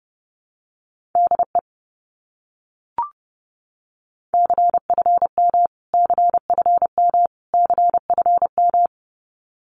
Morse code, heard as a keyed tone, about 30 wpm, 700 Hz. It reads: DE CFM CFM CFM
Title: Abbreviations Repeated 3x - 30wpm